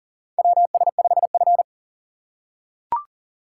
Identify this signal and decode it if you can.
Morse code, heard as a keyed tone, about 40 wpm, 700 Hz. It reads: WS5F
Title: USA Call Signs - 4 Characters - 6x Character Spacing - 40wpm